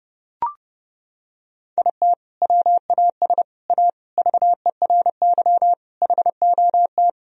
Morse code, heard as a keyed tone, about 30 wpm, 700 Hz. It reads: IT WAS A VERY HOT
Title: Sentences from Top 1000 Words - Review Part 3 - 30wpm